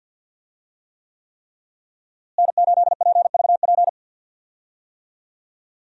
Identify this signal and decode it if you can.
Morse code, heard as a keyed tone, about 48 wpm, 700 Hz. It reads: N8PVP
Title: USA Call Signs - 5 Characters - Repeated 3x - 50wpm